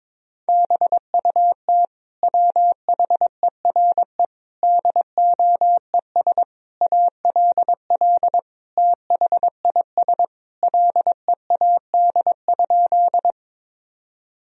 Morse code, heard as a keyed tone, about 22 wpm, 700 Hz. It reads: BUT WHERE DOES ALL THIS LEAD?
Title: Sentences from Top 1000 Words - Review Part 2 - 22wpm